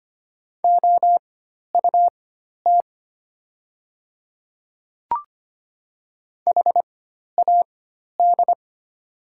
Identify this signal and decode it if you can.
Morse code, heard as a keyed tone, about 25 wpm, spaced as for 12 wpm, 700 Hz. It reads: OUT HAD
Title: 4x Character Spacing - Top 200 Words - 25wpm